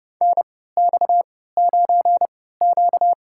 Morse code, heard as a keyed tone, about 30 wpm, spaced as for 17 wpm, 700 Hz. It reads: NX9Q
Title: USA Call Signs - 4 Characters - 3x Character Spacing - 30wpm